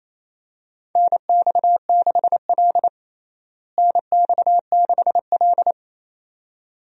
Morse code, heard as a keyed tone, about 27 wpm, 700 Hz. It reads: NX6L NX6L
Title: USA Call Signs - 4 Characters - Repeated 2x - 28wpm